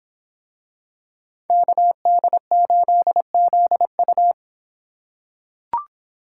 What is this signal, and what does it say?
Morse code, heard as a keyed tone, about 24 wpm, 700 Hz. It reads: KD8ZU